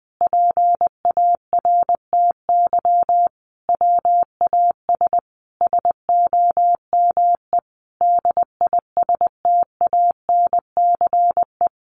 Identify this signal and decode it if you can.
Morse code, heard as a keyed tone, about 20 wpm, 700 Hz. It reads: PARTY WAS SOME DISTANCE
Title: Sentences from Top 1400 Words 20wpm